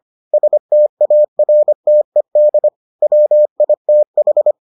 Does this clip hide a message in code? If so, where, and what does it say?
Yes: Morse code from 0.3 to 4.5 s, heard as a keyed tone, about 25 wpm, 600 Hz. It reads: STARTED WITH